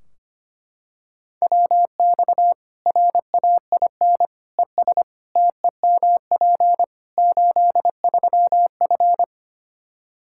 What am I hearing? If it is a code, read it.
Morse code, heard as a keyed tone, about 25 wpm, 700 Hz. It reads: WX RAIN ES TEMP 83F
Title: Practice QSO - Exchange 2 - Info - 25wpm